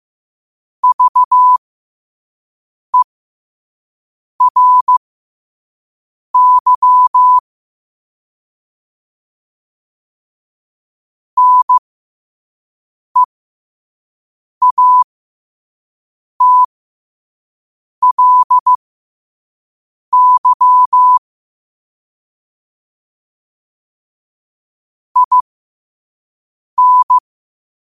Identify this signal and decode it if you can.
Morse code, heard as a keyed tone, about 15 wpm, spaced as for 5 wpm, 1000 Hz. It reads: VERY NEATLY IN